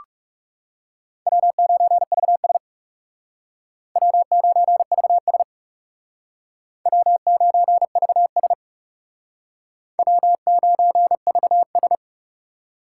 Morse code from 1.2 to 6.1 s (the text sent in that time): W9VS W9VS